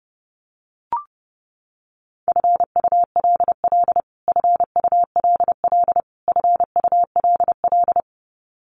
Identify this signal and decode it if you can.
Morse code, heard as a keyed tone, about 30 wpm, 700 Hz. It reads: FULL FULL FULL
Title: Top 500 Words Repeated 30wpm